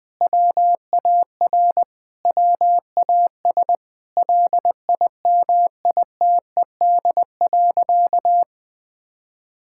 Morse code, heard as a keyed tone, about 21 wpm, 700 Hz. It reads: WAR WAS LIMITED.
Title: Sentences from Top 1000 Words - Review Part 6 - 20wpm